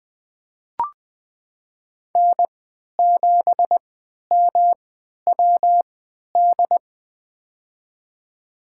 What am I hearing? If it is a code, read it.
Morse code, heard as a keyed tone, about 20 wpm, spaced as for 11 wpm, 700 Hz. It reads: N7MWD
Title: USA Call Signs - 5 Characters - 3x Character Spacing - 20wpm